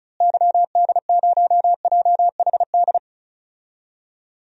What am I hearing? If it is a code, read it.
Morse code, heard as a keyed tone, about 34 wpm, 700 Hz. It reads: YD0JHD
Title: DXCC-100 Call Signs - 6 Characters - Repeated 2x - 35wpm